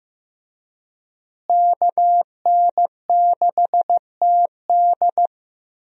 Morse code, heard as a keyed tone, about 15 wpm, 700 Hz. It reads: KN6TD
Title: USA Call Signs - 5 Characters - Repeated 2x - 15wpm